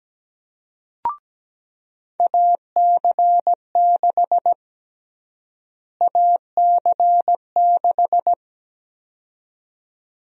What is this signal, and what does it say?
Morse code, heard as a keyed tone, about 17 wpm, 700 Hz. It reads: AC6 AC6
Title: USA Call Sign Prefixes - Repeated 2x - 17wpm